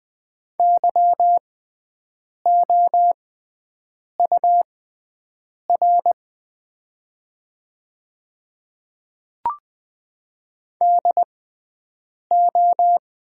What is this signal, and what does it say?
Morse code, heard as a keyed tone, about 20 wpm, spaced as for 7 wpm, 700 Hz. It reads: YOUR DO
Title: 6x Character Spacing - Top 300 Words - 20wpm